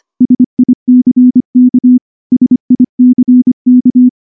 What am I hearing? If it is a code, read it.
Morse code, heard as a keyed tone, about 23 wpm, 250 Hz. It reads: SICK SICK
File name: morse / morse-training-25WPM